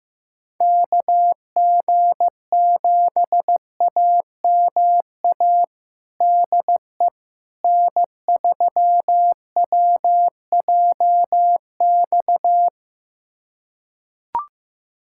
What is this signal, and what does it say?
Morse code, heard as a keyed tone, about 15 wpm, 700 Hz. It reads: KG7AMA DE N3WJX
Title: USA Callsign DE Callsign 15wpm